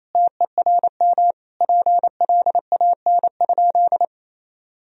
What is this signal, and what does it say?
Morse code, heard as a keyed tone, about 28 wpm, 700 Hz. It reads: TERM PLAN?
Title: Sentences from Top 1000 Words - Review Part 4 - 28wpm